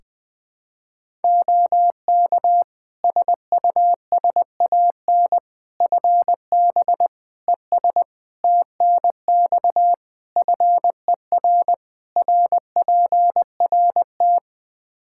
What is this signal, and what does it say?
Morse code, heard as a keyed tone, about 20 wpm, 700 Hz. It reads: OK SUSAN FB ES TNX FER RPRT
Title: Practice QSO - Exchange 2 - Info - 20wpm